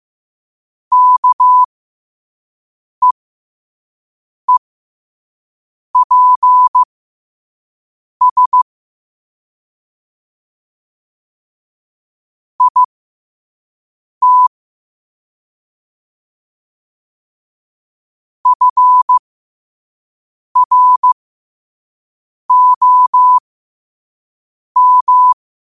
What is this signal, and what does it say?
Morse code, heard as a keyed tone, about 15 wpm, spaced as for 5 wpm, 1000 Hz. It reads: KEEPS IT FROM
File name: qotdfiles / 5wpm